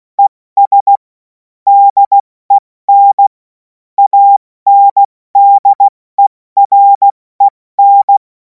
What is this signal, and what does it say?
Morse code, heard as a keyed tone, about 16 wpm, 800 Hz. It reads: ES DEN ANDEREN
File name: Morsekurs / Witze / Witz 07. April 2026 70 F80